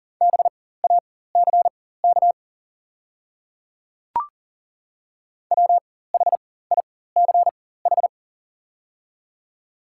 Morse code, heard as a keyed tone, about 41 wpm, spaced as for 19 wpm, 700 Hz. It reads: BACK WHICH
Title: 4x Character Spacing - Top 200 Words - 40wpm